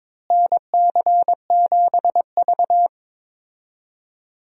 Morse code, heard as a keyed tone, about 22 wpm, 700 Hz. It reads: NC7V
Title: USA Call Signs - 4 Characters - Repeated 2x - 22wpm